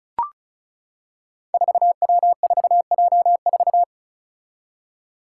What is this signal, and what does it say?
Morse code, heard as a keyed tone, about 36 wpm, 700 Hz. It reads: 4W4J4
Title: Five Challenging Letter-Number Combinations 35wpm